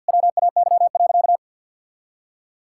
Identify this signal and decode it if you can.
Morse code, heard as a keyed tone, about 50 wpm, 700 Hz. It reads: WAY.
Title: Sentences from Top 1000 Words - Review Part 6 - 50wpm